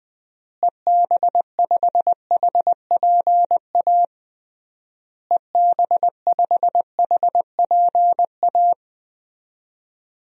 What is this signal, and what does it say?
Morse code, heard as a keyed tone, about 20 wpm, 700 Hz. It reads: EB5HPA EB5HPA
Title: DXCC-100 Call Signs - 6 Characters - Repeated 3x - 20wpm